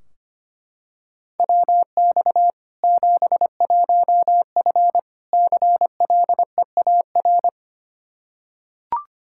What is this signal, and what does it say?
Morse code, heard as a keyed tone, about 25 wpm, 700 Hz. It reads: WX 71F CLEAR
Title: K1USN SST QSO Practice 25wpm